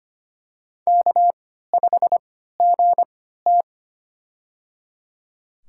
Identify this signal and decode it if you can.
Morse code, heard as a keyed tone, about 25 wpm, spaced as for 14 wpm, 700 Hz. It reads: K5GT
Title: USA Call Signs - 4 Characters - 3x Character Spacing - 25wpm